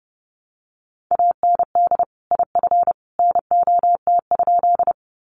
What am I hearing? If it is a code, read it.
Morse code, heard as a keyed tone, about 29 wpm, 700 Hz. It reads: AND IF NOT?
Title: Sentences from Top 600 Words 30wpm